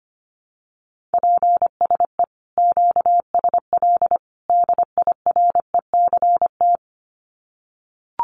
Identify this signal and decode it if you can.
Morse code, heard as a keyed tone, about 25 wpm, 700 Hz. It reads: PSE QSL DIRECT
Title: Common QSO Phrases 25wpm